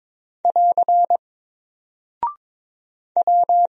Morse code, heard as a keyed tone, about 21 wpm, 700 Hz. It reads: <AR> W
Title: Single Character (Letter:Number:Punctuation:Prosign) Alt 22wpm